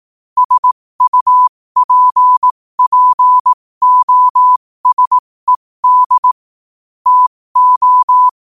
Morse code, heard as a keyed tone, about 18 wpm, spaced as for 15 wpm, 1000 Hz. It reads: SUPPOSED TO
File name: qotdfiles / 15wpm